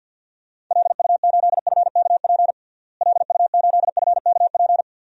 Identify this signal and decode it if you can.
Morse code, heard as a keyed tone, about 49 wpm, 700 Hz. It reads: LU8FKP LU8FKP
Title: DXCC-100 Call Signs - 6 Characters - Repeated 3x - 50wpm